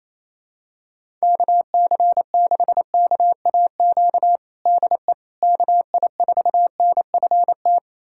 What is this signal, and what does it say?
Morse code, heard as a keyed tone, about 28 wpm, 700 Hz. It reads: KC6KAQ DE KI4NFT
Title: USA Callsign DE Callsign - 6 Characters - 28wpm